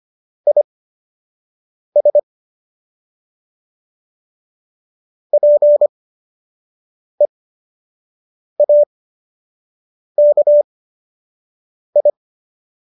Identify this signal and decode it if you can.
Morse code, heard as a keyed tone, about 25 wpm, spaced as for 6 wpm, 590 Hz. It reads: IS PEAKI